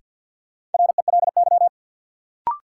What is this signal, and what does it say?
Morse code, heard as a keyed tone, about 53 wpm, 700 Hz. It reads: RELY
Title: Sentences from Top 1200 Words - New Word Drill - 50wpm